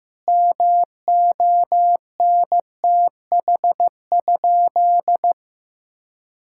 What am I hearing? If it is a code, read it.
Morse code, heard as a keyed tone, about 15 wpm, 700 Hz. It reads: MONTH?
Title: Sentences from Top 1000 Words - Review Part 6 - 15wpm